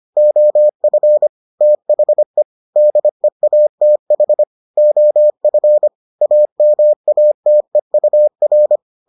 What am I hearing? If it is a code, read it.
Morse code, heard as a keyed tone, about 25 wpm, 600 Hz. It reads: OF THE DEATH OF AMATEUR